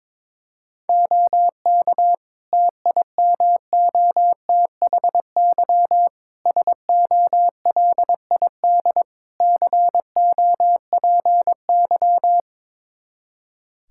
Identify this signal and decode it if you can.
Morse code, heard as a keyed tone, about 22 wpm, 700 Hz. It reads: OK TIMOTHY SOLID COPY
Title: K1USN SST QSO Practice 22wpm